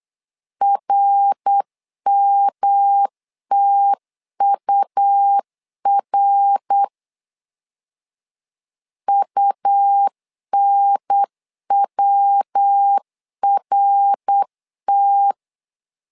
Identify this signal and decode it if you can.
Morse code, heard as a keyed tone, about 8 wpm, 800 Hz. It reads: RMTUR UNWRT
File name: LEZIONI / 3 MWRO / 3 MWR 8wpm 2444